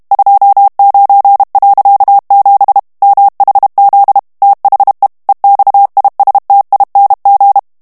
Morse code, heard as a keyed tone, about 32 wpm, 800 Hz. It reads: 29.7 MHZ THE EXISTING
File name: morse mp3 / ANGLAIS texte clair 3 vitesse 30 mots minute